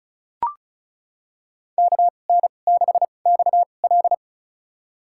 Morse code, heard as a keyed tone, about 34 wpm, spaced as for 25 wpm, 700 Hz. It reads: KN6XL
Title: USA Call Signs - 5 Characters - 2x Character Spacing - 35wpm